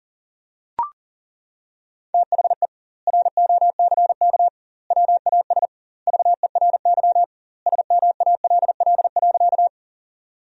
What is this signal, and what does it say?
Morse code, heard as a keyed tone, about 40 wpm, 700 Hz. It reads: THE ROCK WAS VERY SMALL.
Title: 2x Word-Spacing - Sentences from Top 400 Words - 40wpm